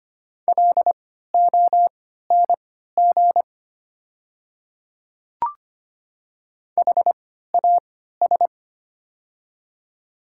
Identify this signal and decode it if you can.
Morse code, heard as a keyed tone, about 25 wpm, spaced as for 14 wpm, 690 Hz. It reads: LONG HAS
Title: 3x Character Spacing - Top 500 Words - 25wpm